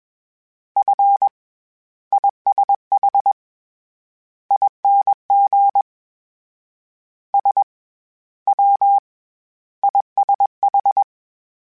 Morse code, heard as a keyed tone, about 21 wpm, 800 Hz. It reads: F ISH ING S W ISH